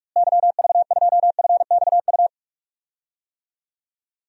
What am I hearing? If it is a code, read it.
Morse code, heard as a keyed tone, about 46 wpm, 700 Hz. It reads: YV2FXU
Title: DXCC-100 Call Signs - 6 Characters - Repeated 2x - 45wpm